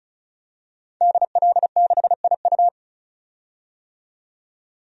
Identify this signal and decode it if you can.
Morse code, heard as a keyed tone, about 34 wpm, 700 Hz. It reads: DL6IU
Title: DXCC-100 Call Signs - 5 Characters - 35wpm